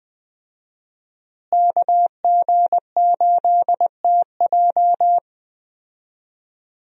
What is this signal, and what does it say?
Morse code, heard as a keyed tone, about 20 wpm, 700 Hz. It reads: KG8TJ
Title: USA Call Signs - 5 Characters - Repeated 2x - 20wpm